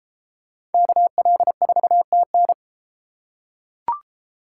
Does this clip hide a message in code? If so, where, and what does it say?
Yes: Morse code from 0.7 to 2.5 s, heard as a keyed tone, about 33 wpm, 700 Hz. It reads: KL4TN